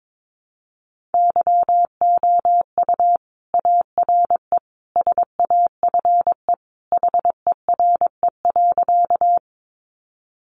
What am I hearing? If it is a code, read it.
Morse code, heard as a keyed tone, about 22 wpm, 700 Hz. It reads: YOU ARE SAFE HERE.
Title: Sentences from Top 900 Words 22wpm